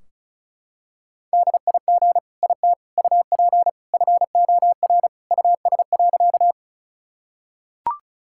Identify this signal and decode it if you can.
Morse code, heard as a keyed tone, about 36 wpm, 700 Hz. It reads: DIG IT UP FOR US.
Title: Sentences from Top 1600 Words 35wpm